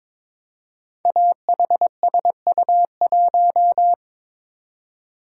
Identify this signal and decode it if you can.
Morse code, heard as a keyed tone, about 22 wpm, 700 Hz. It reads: AHSU1
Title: Code Groups - 5 Letter:Numbers - 22wpm